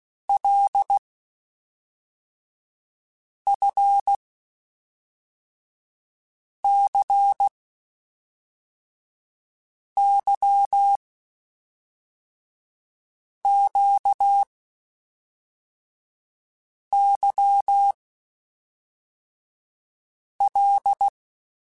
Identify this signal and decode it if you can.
Morse code, heard as a keyed tone, about 16 wpm, 800 Hz. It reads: L F C Y Q Y L
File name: Basic Sets / Basic Letter Groups Collective7 / LFCYQ@5